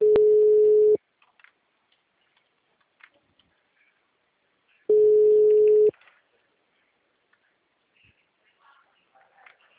Ringback tone (Europe, 425 Hz).
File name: out- -2017